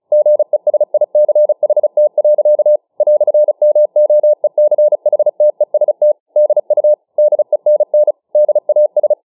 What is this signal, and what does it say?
Morse code, heard as a keyed tone, about 35 wpm, 600 Hz. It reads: GESICHT. "MOECHTEST DU DENN DAS
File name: dk5ke / Morsemogeln 35WpM